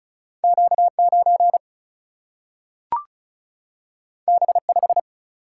Morse code, heard as a keyed tone, about 36 wpm, 700 Hz. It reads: Q 9 B 5
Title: Single-Letter Single-Number 35wpm